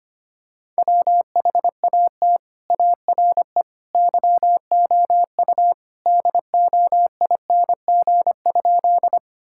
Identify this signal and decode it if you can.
Morse code, heard as a keyed tone, about 25 wpm, 700 Hz. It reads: WHAT ARE YOU DOING?
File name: practice / 25wpm / Sentences with 3-Word Phrases - 25wpm